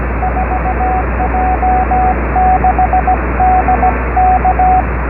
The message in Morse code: 4J6D K